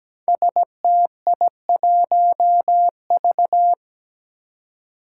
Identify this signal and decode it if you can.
Morse code, heard as a keyed tone, about 17 wpm, 700 Hz. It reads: STI1V
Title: Five Challenging Letter-Number Combinations 17wpm